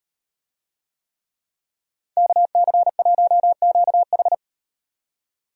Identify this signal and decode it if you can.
Morse code, heard as a keyed tone, about 36 wpm, 700 Hz. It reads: KC1QH